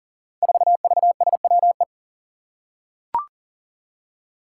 Morse code, heard as a keyed tone, about 40 wpm, 700 Hz. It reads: 4VSWE
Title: Five Challenging Letter-Number Combinations 40wpm